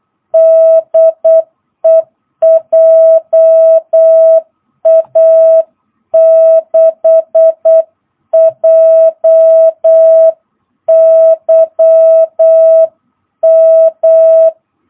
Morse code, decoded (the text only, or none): DE JA6JYM